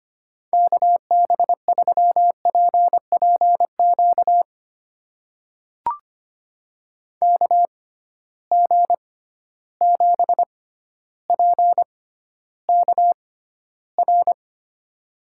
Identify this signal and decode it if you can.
Morse code, heard as a keyed tone, about 25 wpm, spaced as for 13 wpm, 700 Hz. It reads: KB3PPQ KG7PKR
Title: USA Call Signs - 6 Characters - 6x Character Spacing - 25wpm